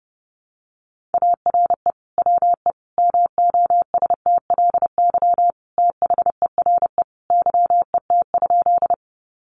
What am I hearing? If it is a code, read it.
Morse code, heard as a keyed tone, about 30 wpm, 700 Hz. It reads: ARE WE MOSTLY THERE YET?
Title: Sentences from Top 1300 Words 30wpm